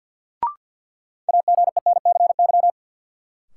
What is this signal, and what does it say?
Morse code, heard as a keyed tone, about 50 wpm, 700 Hz. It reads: AGENCY
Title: Sentences from Top 1300 Words - New Word Drill Alt - 50wpm